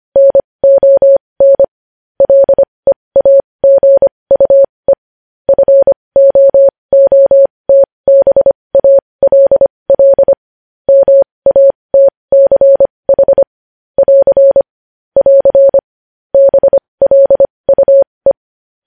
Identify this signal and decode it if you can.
Morse code, heard as a keyed tone, about 25 wpm, spaced as for 20 wpm, 550 Hz. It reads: NON LEAGUE FOOTBALL MATCH <AR> <AR> BLUE